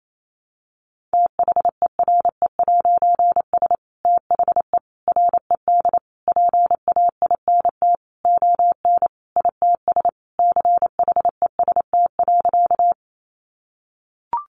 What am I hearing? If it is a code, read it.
Morse code, heard as a keyed tone, about 28 wpm, 700 Hz. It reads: THERE'S THE RED PAINT ON ITS CHEST.
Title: A Sound Of Thunder 28wpm